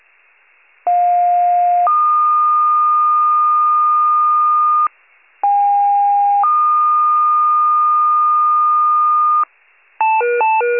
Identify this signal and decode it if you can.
Two-tone page, heard as a one-second tone, then a three-second tone, about 710 then 1220 Hz, about 810 then 1220 Hz. It is heard from 0.9 to 9.4 s.
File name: wcalert